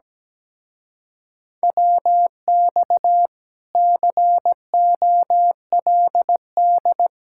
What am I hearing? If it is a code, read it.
Morse code, heard as a keyed tone, about 17 wpm, 700 Hz. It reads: WX COLD
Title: Speed-Racing Common QSO Phrases Encore - 17wpm